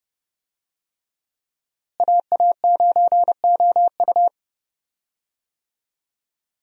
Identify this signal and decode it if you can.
Morse code, heard as a keyed tone, about 30 wpm, 700 Hz. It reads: AA9OU
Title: USA Call Signs - 5 Characters - Repeated 3x - 30wpm